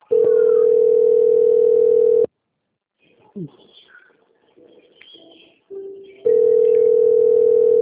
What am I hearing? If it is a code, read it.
Ringback tone (North America), 440 + 480 Hz, with speech.
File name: out- - - - - .14659